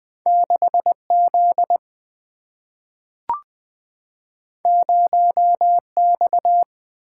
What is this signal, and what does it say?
Morse code, heard as a keyed tone, about 20 wpm, 700 Hz. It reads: 6 Z 0 X
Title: Two Letter-Number Combinations - 20wpm